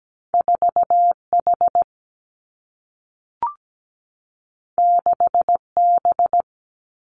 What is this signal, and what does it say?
Morse code, heard as a keyed tone, about 17 wpm, 700 Hz. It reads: 4H 6B
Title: Two Challenging Letter-Number Combinations 17wpm